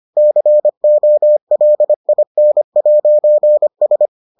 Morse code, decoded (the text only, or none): COLIN'S